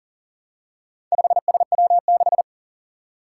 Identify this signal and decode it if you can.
Morse code, heard as a keyed tone, about 40 wpm, 700 Hz. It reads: 5SW6
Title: Four Challenging Letter-Number Combinations 40wpm